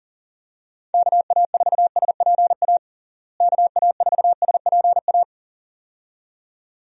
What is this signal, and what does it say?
Morse code, heard as a keyed tone, about 40 wpm, 700 Hz. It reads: KA4SPA KA4SPA
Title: USA Call Signs - 6 Characters - Repeated 2x - 40wpm